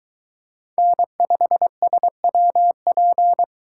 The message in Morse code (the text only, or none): N5SWP